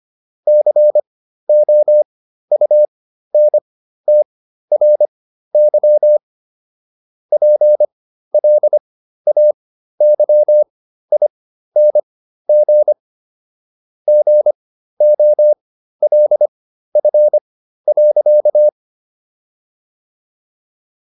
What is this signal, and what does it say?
Morse code, heard as a keyed tone, about 25 wpm, spaced as for 13 wpm, 600 Hz. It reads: COUNTRY PLAYING GOLF.